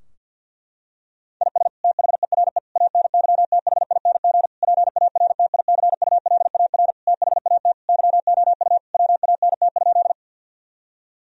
Morse code, heard as a keyed tone, about 50 wpm, 700 Hz. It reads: IS THERE ANYTHING PARTICULAR THAT YOU WANT?
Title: Sentences from Top 1000 Words - Review Part 4 - 50wpm